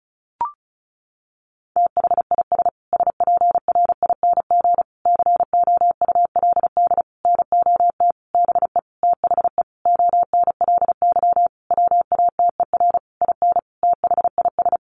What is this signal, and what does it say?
Morse code, heard as a keyed tone, about 34 wpm, 700 Hz. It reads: THIS SPRING COULD NOT BE THE ONLY WATER IN THIS
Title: Sentences from Top 1500 Words 35wpm